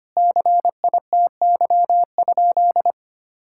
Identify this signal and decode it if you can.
Morse code, heard as a keyed tone, about 25 wpm, 700 Hz. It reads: CITY?
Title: Sentences from Top 1600 Words 25wpm